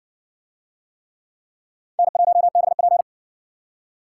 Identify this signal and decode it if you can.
Morse code, heard as a keyed tone, about 52 wpm, 700 Hz. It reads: N1BP